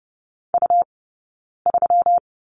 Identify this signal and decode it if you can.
Morse code, heard as a keyed tone, about 30 wpm, 700 Hz. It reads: U 3
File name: Lesson 20d - Number 3 - Two Characters 30wpm